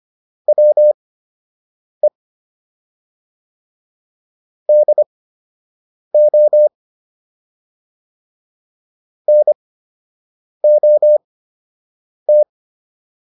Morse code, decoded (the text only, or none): WE DO NOT